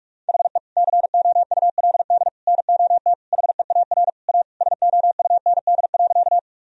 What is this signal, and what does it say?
Morse code, heard as a keyed tone, about 41 wpm, 700 Hz. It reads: HE COULD NOT HEAR A SOUND.
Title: Sentences from Top 1000 Words - Review Part 1 - 45wpm